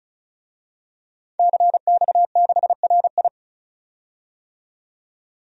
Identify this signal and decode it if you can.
Morse code, heard as a keyed tone, about 34 wpm, 700 Hz. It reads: CX6RI
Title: DXCC-100 Call Signs - 5 Characters - 35wpm